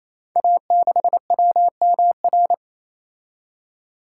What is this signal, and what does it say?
Morse code, heard as a keyed tone, about 28 wpm, 700 Hz. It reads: A6WMR
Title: Code Groups - 5 Letter:Numbers - 28wpm